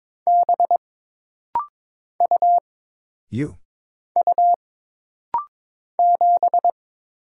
Morse code, heard as a keyed tone, about 22 wpm, 700 Hz. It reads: B U U 7
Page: This rapid-fire timing is geared to push your ICR (Instant Character Recognition) skills.